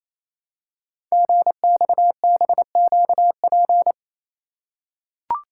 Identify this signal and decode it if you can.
Morse code, heard as a keyed tone, about 29 wpm, 700 Hz. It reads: GXBQP
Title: Code Groups - 5 Letters - 28wpm